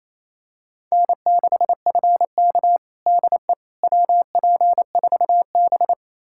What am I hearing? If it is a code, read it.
Morse code, heard as a keyed tone, about 28 wpm, 700 Hz. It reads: N6FK DE WP4B
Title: USA Callsign DE Callsign - 4 Characters - 28wpm